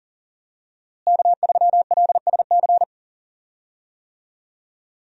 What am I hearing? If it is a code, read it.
Morse code, heard as a keyed tone, about 40 wpm, 700 Hz. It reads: K3LSC
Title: USA Call Signs - 5 Characters - Repeated 3x - 40wpm